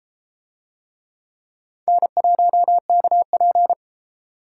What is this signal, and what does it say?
Morse code, heard as a keyed tone, about 30 wpm, 700 Hz. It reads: N1KP